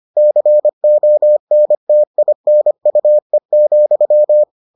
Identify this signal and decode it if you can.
Morse code, heard as a keyed tone, about 25 wpm, 600 Hz. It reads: CONTINUE,